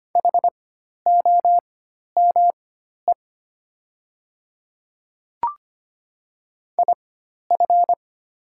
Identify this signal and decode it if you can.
Morse code, heard as a keyed tone, about 25 wpm, spaced as for 12 wpm, 700 Hz. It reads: HOME IF
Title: 4x Character Spacing - Top 200 Words - 25wpm